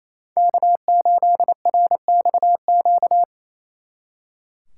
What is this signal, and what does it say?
Morse code, heard as a keyed tone, about 34 wpm, 700 Hz. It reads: K8RXQ
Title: Speed Racing - USA 1x3 Call Signs 28wpm